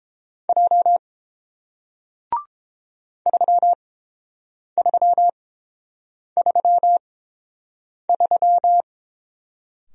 Morse code, heard as a keyed tone, about 30 wpm, spaced as for 7 wpm, 700 Hz. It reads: J 3333